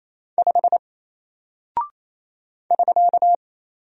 Morse code, heard as a keyed tone, about 27 wpm, 700 Hz. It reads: 5 <SK>
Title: Single Character (Letter:Number:Punctuation:Prosign) 28wpm